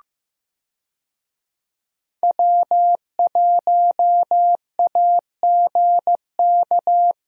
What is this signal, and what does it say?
Morse code, heard as a keyed tone, about 16 wpm, 700 Hz. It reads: W1AGK
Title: USA Call Signs - 5 Characters - Repeated 2x - 15wpm